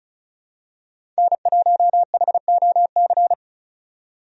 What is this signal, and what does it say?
Morse code, heard as a keyed tone, about 34 wpm, 700 Hz. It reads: N1HOC
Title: USA Call Signs - 5 Characters - 35wpm